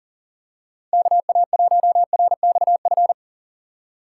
Morse code, heard as a keyed tone, about 40 wpm, 700 Hz. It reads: KA1RXF
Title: USA Call Signs - 6 Characters - 4x Character Spacing - 40wpm